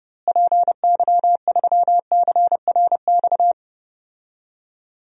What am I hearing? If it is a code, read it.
Morse code, heard as a keyed tone, about 30 wpm, 700 Hz. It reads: PY3CRX
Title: DXCC-100 Call Signs - 6 Characters - Repeated 3x - 30wpm